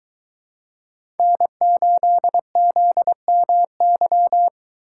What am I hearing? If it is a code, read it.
Morse code, heard as a keyed tone, about 22 wpm, 700 Hz. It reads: N8ZMY